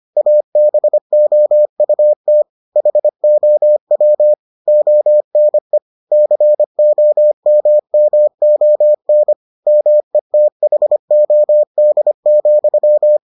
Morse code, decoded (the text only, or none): ABOUT HOW ONE COMMON METHOD,